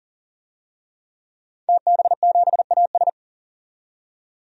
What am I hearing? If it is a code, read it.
Morse code, heard as a keyed tone, about 40 wpm, 700 Hz. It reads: TB7AS
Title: DXCC-100 Call Signs - 5 Characters - 40wpm